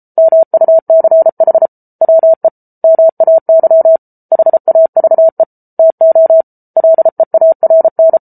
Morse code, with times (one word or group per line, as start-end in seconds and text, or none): 0.2-1.7 MUCH
2.0-2.5 WE
2.8-4.0 MAY
4.3-5.4 HAVE
5.8-6.4 TO
6.8-8.2 LEARN